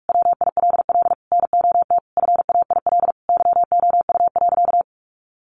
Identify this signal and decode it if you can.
Morse code, heard as a keyed tone, about 44 wpm, 700 Hz. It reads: WILL NOT FAIL YOU.
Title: Sentences from Top 1100 Words 45wpm